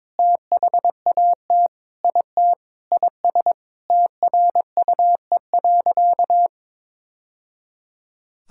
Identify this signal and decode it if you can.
Morse code, heard as a keyed tone, about 22 wpm, 700 Hz. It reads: THAT IT IS TRUE.
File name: practice / 22wpm / Sentences with 3-Word Phrases - 22wpm